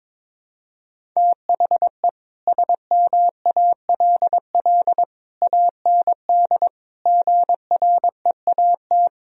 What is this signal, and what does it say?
Morse code, heard as a keyed tone, about 22 wpm, 700 Hz. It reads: THE SMALL AND GREAT
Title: Sentences from Top 1000 Words - Review Part 1 - 22wpm